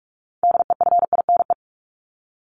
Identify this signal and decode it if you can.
Morse code, heard as a keyed tone, about 44 wpm, 700 Hz. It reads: DEFINE
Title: ED Suffix - Top 100 Words with Base Words 45wpm